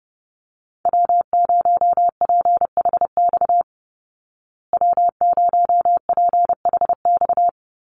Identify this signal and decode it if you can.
Morse code, heard as a keyed tone, about 30 wpm, 700 Hz. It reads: W0PHX W0PHX
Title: Call Signs Repeated 3x - 30wpm